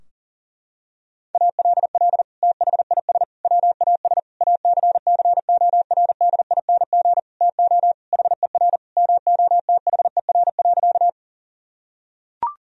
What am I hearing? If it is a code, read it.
Morse code, heard as a keyed tone, about 40 wpm, 700 Hz. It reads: ALL THIS WAS ACCORDING TO HER MOTHER.
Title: Sentences from Top 1000 Words - Review Part 4 - 40wpm